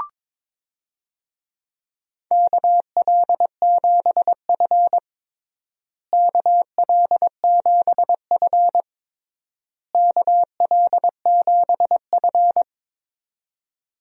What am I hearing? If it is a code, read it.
Morse code, heard as a keyed tone, about 22 wpm, 700 Hz. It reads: KL7F KL7F KL7F
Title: DXCC-100 Call Signs - 4 Characters - Repeated 3x - 22wpm